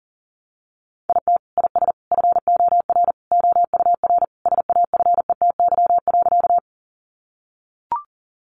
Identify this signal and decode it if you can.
Morse code, heard as a keyed tone, about 40 wpm, 700 Hz. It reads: IT IS FOR OUR SAFETY.
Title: Sentences from Top 700 Words 40wpm